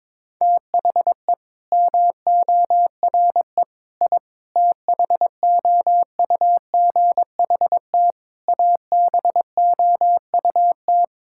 Morse code, heard as a keyed tone, about 22 wpm, 700 Hz. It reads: THE MORE I THOUGHT ABOUT
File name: Sets of 9 Words - 22wpm